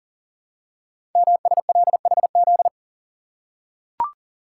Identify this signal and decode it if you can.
Morse code, heard as a keyed tone, about 43 wpm, 700 Hz. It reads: MSLHZ
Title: Code Groups - 5 Letter:Numbers - 40wpm